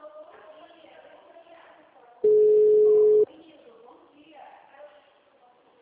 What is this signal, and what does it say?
Ringback tone (Europe, 425 Hz), with speech.
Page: Dracena (Ura-Reversa)